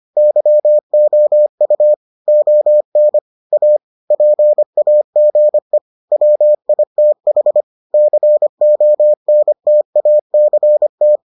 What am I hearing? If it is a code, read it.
Morse code, heard as a keyed tone, about 25 wpm, 600 Hz. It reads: YOU ON A PAGE WITH CONTACT